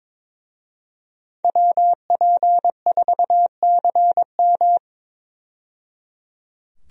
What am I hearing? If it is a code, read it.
Morse code, heard as a keyed tone, about 22 wpm, 700 Hz. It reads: WP4CM
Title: USA Call Signs - 5 Characters - 22wpm